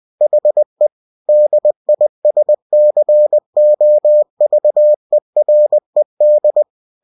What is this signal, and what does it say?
Morse code, heard as a keyed tone, about 20 wpm, 600 Hz. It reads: HE DISCOVERED
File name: cw-books / winnie-the-pooh / winnie-20wpm-page-0093